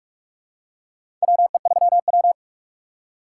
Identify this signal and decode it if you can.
Morse code, heard as a keyed tone, about 46 wpm, 700 Hz. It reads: WE3W
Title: USA Call Signs - 4 Characters - 2x Character Spacing - 45wpm